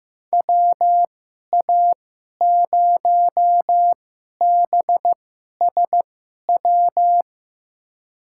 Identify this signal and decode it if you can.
Morse code, heard as a keyed tone, about 15 wpm, spaced as for 11 wpm, 700 Hz. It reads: WA0BSW
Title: USA Call Signs - 2x Character Spacing - 15wpm